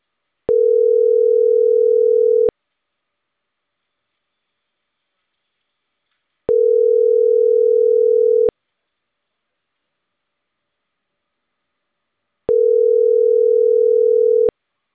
Ringback tone (North America), 440 + 480 Hz.